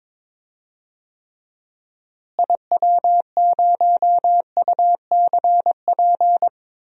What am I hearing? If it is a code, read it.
Morse code, heard as a keyed tone, about 22 wpm, 700 Hz. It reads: IW0UCP
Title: DXCC-100 Call Signs - 6 Characters - Repeated 3x - 22wpm